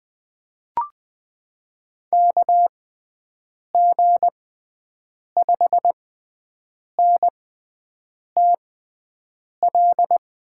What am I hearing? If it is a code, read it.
Morse code, heard as a keyed tone, about 20 wpm, spaced as for 7 wpm, 700 Hz. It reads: KG5NTL
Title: USA Call Signs - 6 Characters - 6x Character Spacing - 20wpm